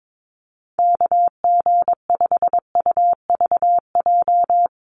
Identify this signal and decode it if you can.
Morse code, heard as a keyed tone, about 22 wpm, 700 Hz. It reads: KG5UVJ
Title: USA Call Signs Repeated 3x 22wpm